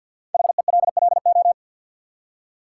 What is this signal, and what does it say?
Morse code, heard as a keyed tone, about 53 wpm, 700 Hz. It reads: HELLO
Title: Five Letter Words - 50wpm